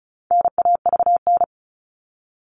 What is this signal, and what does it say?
Morse code, heard as a keyed tone, about 34 wpm, 700 Hz. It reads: NAVN
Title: All Cut Numbers - 4-Digit 35wpm